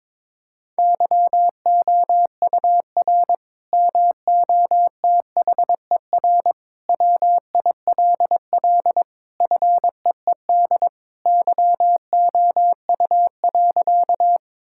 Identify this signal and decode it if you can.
Morse code, heard as a keyed tone, about 21 wpm, 700 Hz. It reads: YOUR MOTHER WILL FEED YOU.
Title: Sentences from Top 1600 Words 22wpm